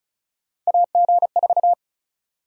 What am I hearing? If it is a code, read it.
Morse code, heard as a keyed tone, about 34 wpm, 700 Hz. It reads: AG4
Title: USA Call Sign Prefixes - 35wpm